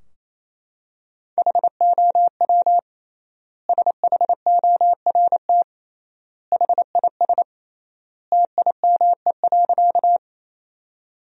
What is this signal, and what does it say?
Morse code, heard as a keyed tone, about 28 wpm, 700 Hz. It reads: HOW SHORT HIS TIME.
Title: 3x Word-Spacing - Sentences from Top 400 Words - 28wpm